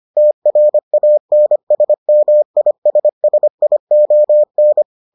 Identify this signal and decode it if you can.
Morse code, heard as a keyed tone, about 25 wpm, 600 Hz. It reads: TRANSMISSION